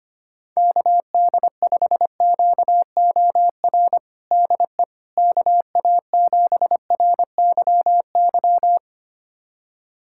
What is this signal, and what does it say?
Morse code, heard as a keyed tone, about 25 wpm, 700 Hz. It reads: KD5QOR DE KA7RYY
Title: USA Callsign DE Callsign - 6 Characters - 25wpm